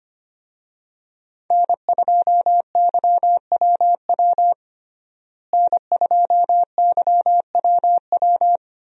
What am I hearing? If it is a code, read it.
Morse code, heard as a keyed tone, about 25 wpm, 700 Hz. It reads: N2YWW N2YWW
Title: USA Call Signs - 5 Characters - Repeated 2x - 25wpm